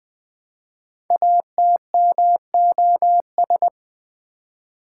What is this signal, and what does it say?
Morse code, heard as a keyed tone, about 20 wpm, 700 Hz. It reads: ATMOS
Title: Build A Word - 5 Letters At A Time - 10-Letter Words - 20wpm